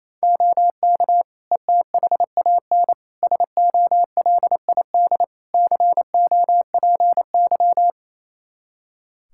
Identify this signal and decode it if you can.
Morse code, heard as a keyed tone, about 28 wpm, 700 Hz. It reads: OK ETHAN SOLID COPY
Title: Practice QSO - Exchange 3 - More Info - 28wpm